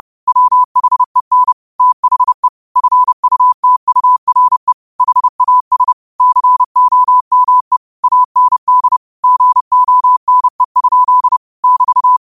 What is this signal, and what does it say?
Morse code, heard as a keyed tone, about 30 wpm, 1000 Hz. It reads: WHEN THE FUTURE HAS COME AND GONE? <BT>
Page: New quotes every day in morse code at 30 Words per minute.